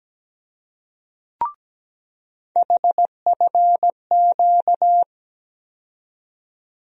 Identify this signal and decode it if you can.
Morse code, heard as a keyed tone, about 17 wpm, 700 Hz. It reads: HFQ
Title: USA Call Sign Postfixes - 3 Letters 17wpm